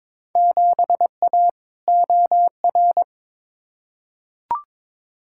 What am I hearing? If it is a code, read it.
Morse code, heard as a keyed tone, about 21 wpm, 700 Hz. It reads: 7A OR
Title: Field Day Exchanges Encore - 22wpm